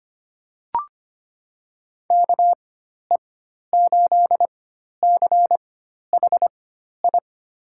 Morse code, heard as a keyed tone, about 25 wpm, spaced as for 12 wpm, 700 Hz. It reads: KE8CHI
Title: USA Call Signs - 6 Characters - 4x Character Spacing - 25wpm